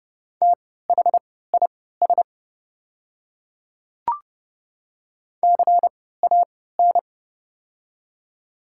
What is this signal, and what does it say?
Morse code, heard as a keyed tone, about 29 wpm, spaced as for 17 wpm, 700 Hz. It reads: THIS CAN
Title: 3x Character Spacing - Top 200 Words - 30wpm